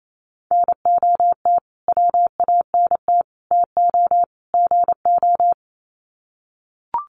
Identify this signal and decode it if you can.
Morse code, heard as a keyed tone, about 28 wpm, 700 Hz. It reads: NOT WANT TO GO
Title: Sets of 4 Words 28wpm